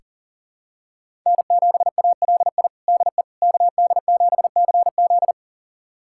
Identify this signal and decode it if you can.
Morse code, heard as a keyed tone, about 40 wpm, 700 Hz. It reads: N7ALI DE KD7CZ
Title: USA Callsign DE Callsign - 5 Characters - 40wpm